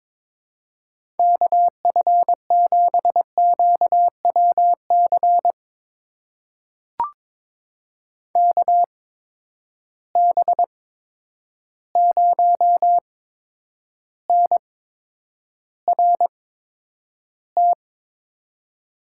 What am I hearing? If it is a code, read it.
Morse code, heard as a keyed tone, about 22 wpm, spaced as for 9 wpm, 700 Hz. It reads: KF7QWC KB0NRT
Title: USA Call Signs - 6 Characters - 8x Character Spacing - 22wpm